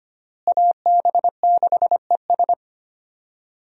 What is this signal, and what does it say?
Morse code, heard as a keyed tone, about 25 wpm, 700 Hz. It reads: AB6ES
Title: USA Call Signs - 5 Characters - Repeated 2x - 25wpm